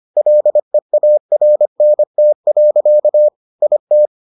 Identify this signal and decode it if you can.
Morse code, heard as a keyed tone, about 25 wpm, 600 Hz. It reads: LEARNT. IT